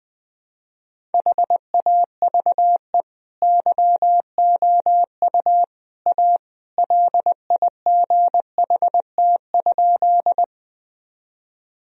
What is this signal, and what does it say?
Morse code, heard as a keyed tone, about 20 wpm, 700 Hz. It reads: HAVE YOU A LIGHT?
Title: Sentences from Top 1000 Words - Review Part 1 - 20wpm